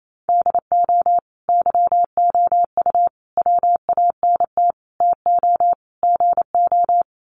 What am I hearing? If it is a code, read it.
Morse code, heard as a keyed tone, about 28 wpm, 700 Hz. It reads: DO YOU WANT TO GO
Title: Sets of 5 Words Encore2 28wpm